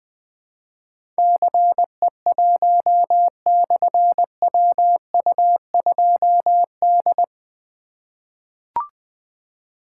Morse code, heard as a keyed tone, about 20 wpm, 700 Hz. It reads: CE1/WU2D
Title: Difficult Call Signs 20wpm